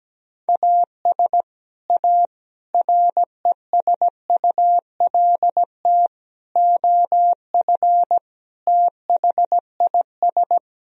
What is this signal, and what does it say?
Morse code, heard as a keyed tone, about 17 wpm, 700 Hz. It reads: AS A RESULT OF THIS
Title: Speed-Racing - Sets of 5 Words 17wpm